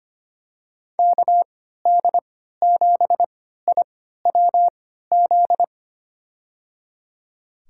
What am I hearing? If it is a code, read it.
Morse code, heard as a keyed tone, about 25 wpm, spaced as for 14 wpm, 700 Hz. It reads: KD7IWZ
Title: USA Call Signs - 6 Characters - 3x Character Spacing - 25wpm